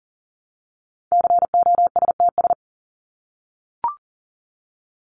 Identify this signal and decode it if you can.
Morse code, heard as a keyed tone, about 40 wpm, 700 Hz. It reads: COSTS
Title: ES:S Suffix - Top 100 Words with Base Words 40wpm